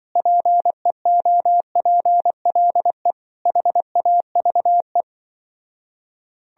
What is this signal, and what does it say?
Morse code, heard as a keyed tone, about 23 wpm, 700 Hz. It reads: PEOPLE HAVE
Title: Speed-Racing - Sets of 5 Words 20wpm